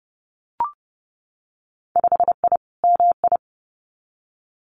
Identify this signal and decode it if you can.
Morse code, heard as a keyed tone, about 30 wpm, 700 Hz. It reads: 5I MI
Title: Winter Field Day Exchanges Encore 30wpm